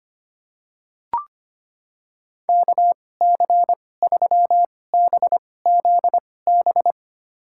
Morse code, heard as a keyed tone, about 25 wpm, spaced as for 18 wpm, 700 Hz. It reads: KC3BZB
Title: USA Call Signs - 6 Characters - 2x Character Spacing - 25wpm